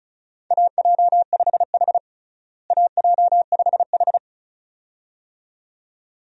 Morse code, heard as a keyed tone, about 36 wpm, 700 Hz. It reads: AJ5H AJ5H
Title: USA Call Signs - 4 Characters - Repeated 2x - 35wpm